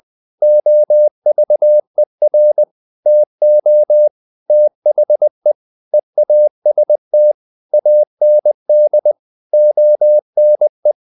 Morse code, heard as a keyed tone, about 20 wpm, 600 Hz. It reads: OVER TO THE EAST AND ONE
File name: the old man and the sea part 2 20wpm